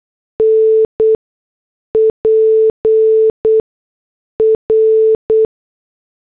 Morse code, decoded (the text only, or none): NPR